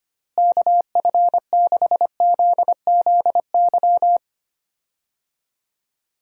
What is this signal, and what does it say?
Morse code, heard as a keyed tone, about 25 wpm, 700 Hz. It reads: KF6ZZY
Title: USA Call Signs - 6 Characters - Repeated 3x - 25wpm